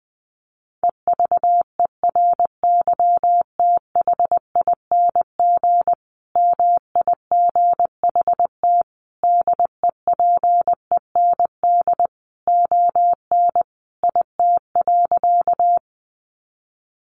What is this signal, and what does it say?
Morse code, heard as a keyed tone, about 20 wpm, 700 Hz. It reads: EVERYTHING MIGHT DEPEND ON IT.
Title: Sentences from Top 1300 Words 20wpm